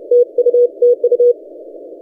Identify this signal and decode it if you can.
Morse code, heard as a keyed tone, about 31 wpm, 480 Hz. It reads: TU TU
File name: posts / vk6isl-oc-294-30cw- -1416z